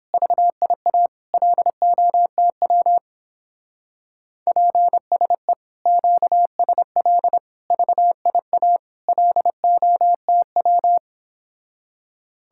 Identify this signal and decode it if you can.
Morse code, heard as a keyed tone, about 31 wpm, 700 Hz. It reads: VIA LOTW PSE QSL VIA LOTW